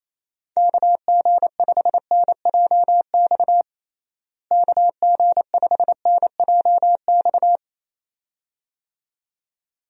Morse code, heard as a keyed tone, about 28 wpm, 700 Hz. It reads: KG5NJX KG5NJX
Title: USA Call Signs - 6 Characters - Repeated 2x - 28wpm